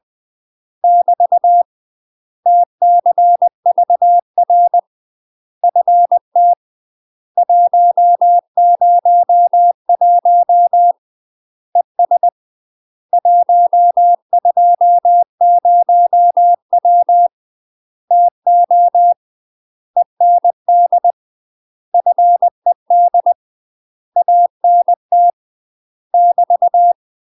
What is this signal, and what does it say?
Morse code, heard as a keyed tone, about 20 wpm, 700 Hz. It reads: <BT> TCVR FT 101 ES 120W TO END FED ANT <BT>
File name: UK MASTER QSOs TRAINING / 20WPM / UK MASTER QSOs 0028